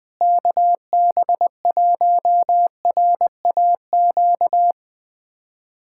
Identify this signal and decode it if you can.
Morse code, heard as a keyed tone, about 20 wpm, 700 Hz. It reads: KB1RAQ
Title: USA Callsign DE Callsign 20wpm